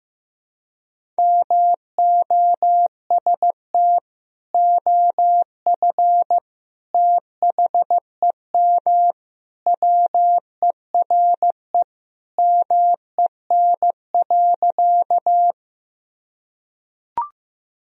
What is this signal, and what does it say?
Morse code, heard as a keyed tone, about 15 wpm, 700 Hz. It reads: MOST OF THEM WERE MEN.
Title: Sentences from Top 1000 Words - Review Part 1 - 15wpm